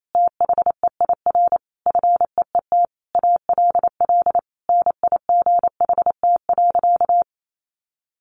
Morse code, heard as a keyed tone, about 28 wpm, 700 Hz. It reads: THEIR FEET ALL NIGHT.
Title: Sentences from Top 1500 Words 28wpm